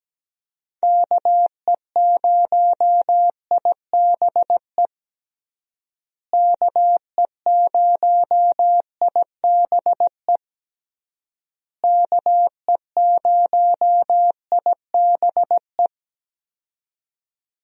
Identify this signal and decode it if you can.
Morse code, heard as a keyed tone, about 17 wpm, 700 Hz. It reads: KE0IBE KE0IBE KE0IBE
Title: USA Call Signs - 6 Characters - Repeated 3x - 17wpm